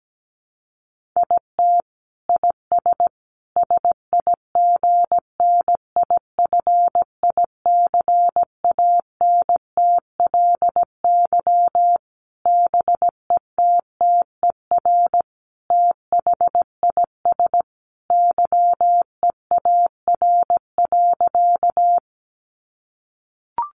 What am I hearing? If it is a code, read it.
Morse code, heard as a keyed tone, about 17 wpm, 700 Hz. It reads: IT IS SIGNIFICANTLY BETTER THIS YEAR.
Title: Sentences from Top 1500 Words 17wpm